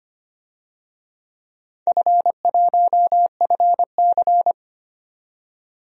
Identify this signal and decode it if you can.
Morse code, heard as a keyed tone, about 25 wpm, 700 Hz. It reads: F1FC
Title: DXCC-100 Call Signs - 4 Characters - 25wpm